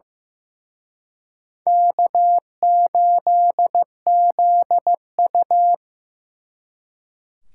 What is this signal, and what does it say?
Morse code, heard as a keyed tone, about 18 wpm, 700 Hz. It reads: K8ZU
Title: Speed Racing - USA 1x2 Call Signs 15wpm